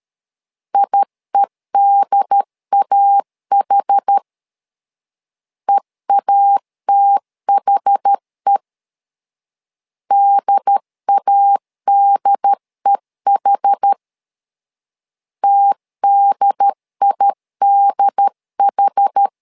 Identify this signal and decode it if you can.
Morse code, heard as a keyed tone, about 13 wpm, 800 Hz. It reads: IEDAH EATHE DADEH TDIDH
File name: LEZIONI / 2 UHDN / 2 UHD 12wpm 1540